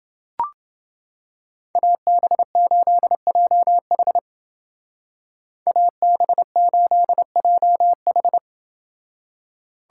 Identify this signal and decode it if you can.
Morse code, heard as a keyed tone, about 28 wpm, 700 Hz. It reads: AB8JH AB8JH